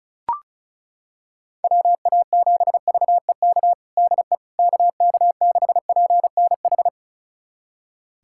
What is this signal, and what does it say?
Morse code, heard as a keyed tone, about 36 wpm, 700 Hz. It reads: WA7VEK DE KK6PNH
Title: USA Callsign DE Callsign - 6 Characters - 35wpm